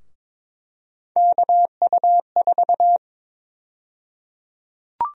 Morse code, heard as a keyed tone, about 22 wpm, 700 Hz. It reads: KU4
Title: USA Call Sign Prefixes - 22wpm